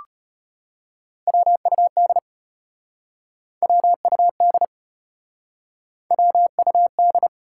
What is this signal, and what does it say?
Morse code, heard as a keyed tone, about 33 wpm, 700 Hz. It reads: WUD WUD WUD